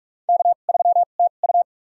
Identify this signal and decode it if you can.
Morse code, heard as a keyed tone, about 46 wpm, spaced as for 33 wpm, 700 Hz. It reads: K3TU
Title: USA Call Signs - 4 Characters - 2x Character Spacing - 45wpm